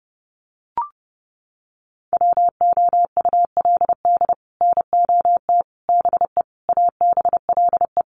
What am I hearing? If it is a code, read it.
Morse code, heard as a keyed tone, about 30 wpm, 700 Hz. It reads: WOULD NOT BE ABLE
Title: Sets of 5 Words 30wpm